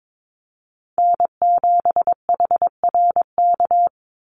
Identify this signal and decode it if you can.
Morse code, heard as a keyed tone, about 22 wpm, 700 Hz. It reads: N7HRK
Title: Call Signs Repeated 2x - 22wpm